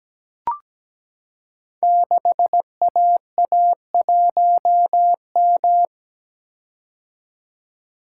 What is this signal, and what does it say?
Morse code, heard as a keyed tone, about 17 wpm, 700 Hz. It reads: 6AA1M
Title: Code Groups - 5 Letter:Numbers - 17wpm